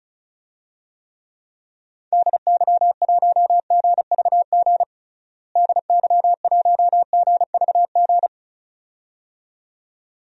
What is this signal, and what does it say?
Morse code, heard as a keyed tone, about 36 wpm, 700 Hz. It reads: DY1GVG DY1GVG
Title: DXCC-100 Call Signs - 6 Characters - Repeated 2x - 35wpm